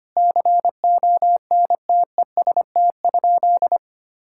Morse code, heard as a keyed tone, about 25 wpm, 700 Hz. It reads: CONTEST?
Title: Sentences from Top 1600 Words 25wpm